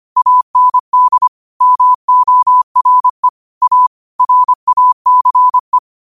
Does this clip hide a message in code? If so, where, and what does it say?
Yes: Morse code from 0.2 to 5.8 s, heard as a keyed tone, about 25 wpm, 1000 Hz. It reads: AND MORE A RACE